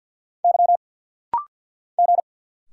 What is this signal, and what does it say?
Morse code, heard as a keyed tone, about 50 wpm, 690 Hz. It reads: Y G
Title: Single Letter-Number Alt - Rapid-Fire 50wpm